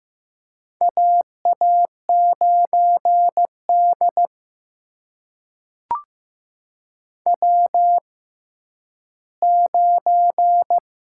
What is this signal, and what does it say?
Morse code, heard as a keyed tone, about 15 wpm, spaced as for 10 wpm, 700 Hz. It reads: AA9D W9
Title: USA Call Signs - 4 Characters - 6x Character Spacing - 15wpm